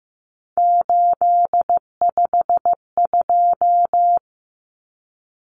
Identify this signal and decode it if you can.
Morse code, heard as a keyed tone, about 15 wpm, 700 Hz. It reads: 852
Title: Common Cut Numbers - 3-Digit 15wpm